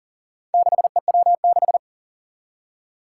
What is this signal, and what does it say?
Morse code, heard as a keyed tone, about 40 wpm, 700 Hz. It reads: 6EW6
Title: Four Challenging Letter-Number Combinations 40wpm